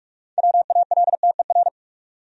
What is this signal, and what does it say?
Morse code, heard as a keyed tone, about 44 wpm, 700 Hz. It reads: WALTER
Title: USA Common Names - 45wpm